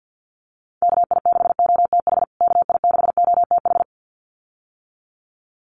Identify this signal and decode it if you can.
Morse code, heard as a keyed tone, about 50 wpm, 700 Hz. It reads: KI6OTH KI6OTH
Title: USA Call Signs Repeated 2x 50wpm